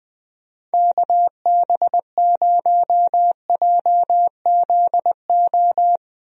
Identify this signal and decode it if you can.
Morse code, heard as a keyed tone, about 20 wpm, 700 Hz. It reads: KB0JZO
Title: USA Call Signs - 6 Characters - Repeated 2x - 20wpm